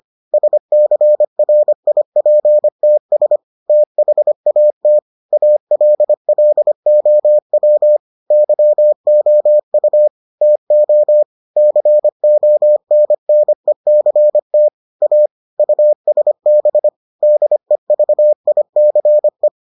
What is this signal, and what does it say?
Morse code, heard as a keyed tone, about 25 wpm, 600 Hz. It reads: SCRIPTS THAT ALLOW YOU TO CONNECT A USB DEVICE